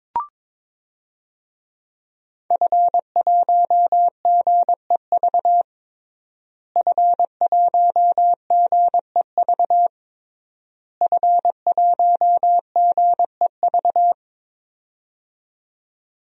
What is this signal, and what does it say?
Morse code, heard as a keyed tone, about 22 wpm, 700 Hz. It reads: F1GEV F1GEV F1GEV
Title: DXCC-100 Call Signs - 5 Characters - Repeated 3x - 22wpm